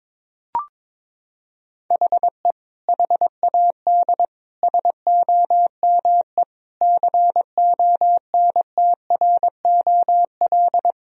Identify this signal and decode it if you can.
Morse code, heard as a keyed tone, about 22 wpm, 700 Hz. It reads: HE HAD SOME CONTROL
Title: Sentences from Top 1000 Words - Review Part 3 - 22wpm